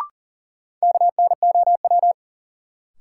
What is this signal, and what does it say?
Morse code, heard as a keyed tone, about 40 wpm, 700 Hz. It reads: KNOW
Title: Sentences from Top 100 Words - New Word Drill - 40wpm